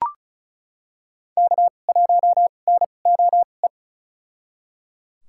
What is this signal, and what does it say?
Morse code, heard as a keyed tone, about 34 wpm, spaced as for 25 wpm, 700 Hz. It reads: K1NOE
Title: USA Call Signs - 5 Characters - 2x Character Spacing - 35wpm